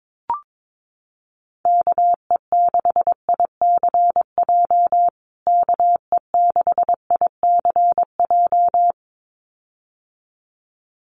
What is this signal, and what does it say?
Morse code, heard as a keyed tone, about 22 wpm, 700 Hz. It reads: KE6ICJ KE6ICJ
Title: USA Call Signs Repeated 2x 22wpm